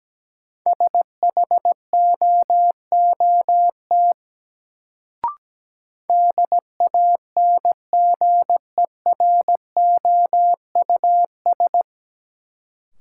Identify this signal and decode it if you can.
Morse code, heard as a keyed tone, about 17 wpm, 700 Hz. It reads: SHOOT DANGEROUS
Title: Sentences from Top 1200 Words - New Word Drill Alt - 17wpm